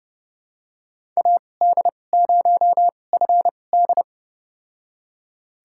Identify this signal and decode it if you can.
Morse code, heard as a keyed tone, about 30 wpm, spaced as for 22 wpm, 700 Hz. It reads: AD0FD
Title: USA Call Signs - 5 Characters - 2x Character Spacing - 30wpm